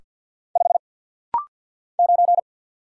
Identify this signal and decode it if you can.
Morse code, heard as a keyed tone, about 49 wpm, 700 Hz. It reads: 5 9
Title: Single Letter-Number - Rapid-Fire 50wpm